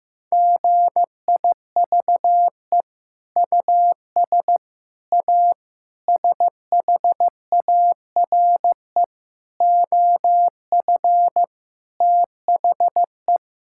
Morse code, heard as a keyed tone, about 16 wpm, 700 Hz. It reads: GIVE US A SHARE OF THE
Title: Sentences from Top 1000 Words - Review Part 6 - 15wpm